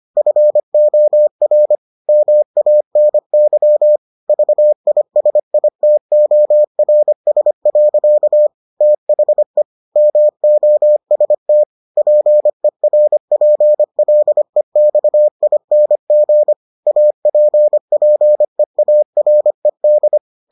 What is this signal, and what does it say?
Morse code, heard as a keyed tone, about 25 wpm, 600 Hz. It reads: FOR MANY VISITORS. THE MOST PERPLEXING APPEARED